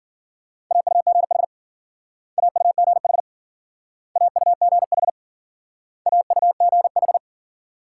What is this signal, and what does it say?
Morse code, heard as a keyed tone, about 50 wpm, 700 Hz. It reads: AUGH AUGH AUGH AUGH